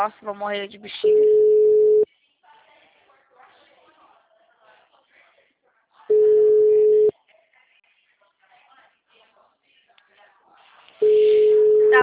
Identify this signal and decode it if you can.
Ringback tone (Europe, 425 Hz), with speech.